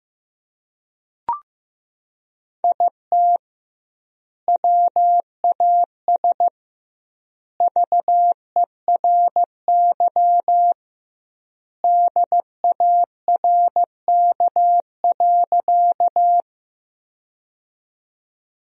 Morse code, heard as a keyed tone, about 15 wpm, 700 Hz. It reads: IT WAS VERY DARK.
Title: 2x Word-Spacing - Sentences from Top 500 Words - 15wpm